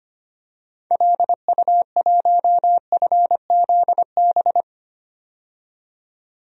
Morse code, heard as a keyed tone, about 25 wpm, 700 Hz. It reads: LU1FZB
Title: DXCC-100 Call Signs - 6 Characters - 25wpm